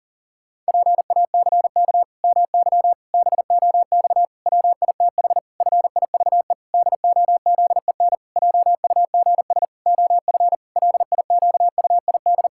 Morse code, heard as a keyed tone, about 40 wpm, 700 Hz. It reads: PACK MY BOX WITH FIVE DOZEN JUGS OF LIQUID
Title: Pangram Challenge v2 - 40wpm